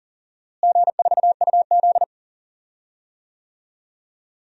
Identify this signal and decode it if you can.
Morse code, heard as a keyed tone, about 40 wpm, 700 Hz. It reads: G4UZ
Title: DXCC-100 Call Signs - 4 Characters - Repeated 2x - 40wpm